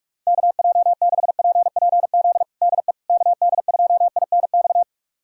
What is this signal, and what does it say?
Morse code, heard as a keyed tone, about 47 wpm, 700 Hz. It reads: KJ6PPZ DE KD2INX
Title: USA Callsign DE Callsign 45wpm